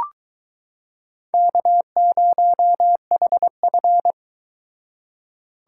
Morse code, heard as a keyed tone, about 22 wpm, 700 Hz. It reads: K0HF